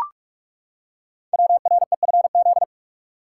Morse code, heard as a keyed tone, about 44 wpm, 700 Hz. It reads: WREFZ
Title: Code Groups - 5 Letter:Numbers - 45wpm